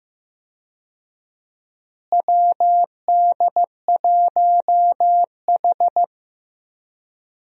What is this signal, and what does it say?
Morse code, heard as a keyed tone, about 16 wpm, 700 Hz. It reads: WD1H
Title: USA Call Signs - 4 Characters - Repeated 2x - 15wpm